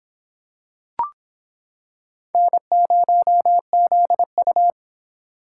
Morse code, heard as a keyed tone, about 26 wpm, 700 Hz. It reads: N0ZU